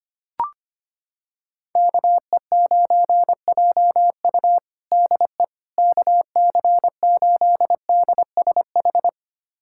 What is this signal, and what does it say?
Morse code, heard as a keyed tone, about 25 wpm, 700 Hz. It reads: KE9JU DE KC8DSH
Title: USA Callsign DE Callsign 25wpm